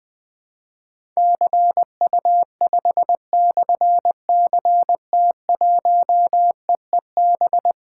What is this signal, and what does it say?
Morse code, heard as a keyed tone, about 21 wpm, 700 Hz. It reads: CU5/CT1EEB
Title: Difficult Call Signs 20wpm